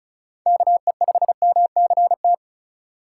Morse code, heard as a keyed tone, about 36 wpm, 700 Hz. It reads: KE5MCT
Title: USA Call Signs - 2x Character Spacing - 35wpm